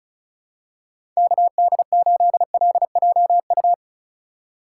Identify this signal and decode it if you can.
Morse code, heard as a keyed tone, about 36 wpm, 700 Hz. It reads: KD8LJU
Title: USA Call Signs - 6 Characters - 2x Character Spacing - 35wpm